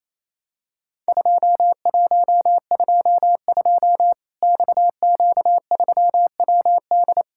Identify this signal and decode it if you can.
Morse code, heard as a keyed tone, about 28 wpm, 700 Hz. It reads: 2122 XQ3WD
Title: Worked All Europe DX Contest - QTC Exchange - 28wpm